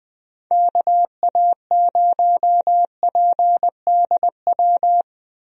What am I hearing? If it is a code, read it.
Morse code, heard as a keyed tone, about 21 wpm, 700 Hz. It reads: KA0PDW
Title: USA Call Signs - 6 Characters - Repeated 3x - 20wpm